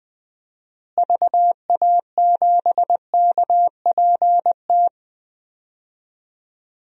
Morse code, heard as a keyed tone, about 20 wpm, 700 Hz. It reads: VA7KPT
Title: DXCC-100 Call Signs - 6 Characters - Repeated 2x - 20wpm